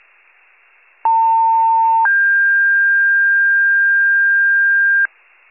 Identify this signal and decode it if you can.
Two-tone page, heard as a one-second tone, then a three-second tone, about 900 then 1600 Hz, with noise. It is heard from 1.0 to 5.1 s.
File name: wcalert